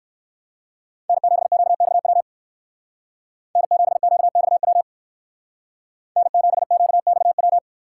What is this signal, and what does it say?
Morse code, heard as a keyed tone, about 54 wpm, 700 Hz. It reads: N7QXW N7QXW N7QXW
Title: Speed Racing - USA 1x3 Call Signs 45wpm